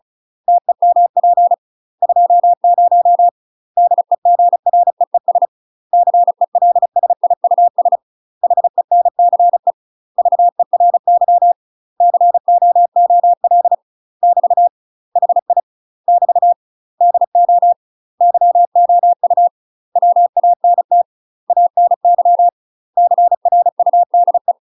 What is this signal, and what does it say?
Morse code, heard as a keyed tone, about 35 wpm, 700 Hz. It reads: TEMP 20 DEGREES CELSIUS HENCE VERY COOL <BT> HI <BT> DO YOU WANT ANY CRUDE
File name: UK MASTER QSOs TRAINING/35WPM